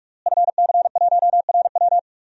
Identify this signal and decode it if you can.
Morse code, heard as a keyed tone, about 44 wpm, 700 Hz. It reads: FC1RW
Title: DXCC-100 Call Signs - 5 Characters - Repeated 3x - 45wpm